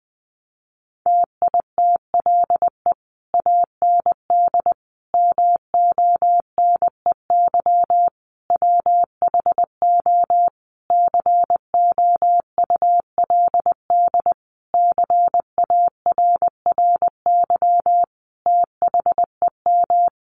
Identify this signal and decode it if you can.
Morse code, heard as a keyed tone, about 20 wpm, 700 Hz. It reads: TITLE AND MONEY WHO COULD CARRY THEM
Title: Sentences from Top 1200 Words 20wpm